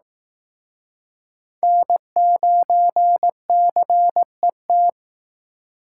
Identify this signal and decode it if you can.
Morse code, heard as a keyed tone, about 18 wpm, 700 Hz. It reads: N9CET
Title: Speed Racing - USA Call Signs 15wpm